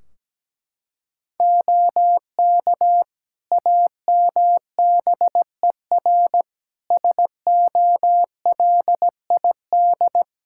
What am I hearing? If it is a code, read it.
Morse code, heard as a keyed tone, about 17 wpm, 700 Hz. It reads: OK AMBER SOLID
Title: Practice QSO - Exchange 3 - More Info - 17wpm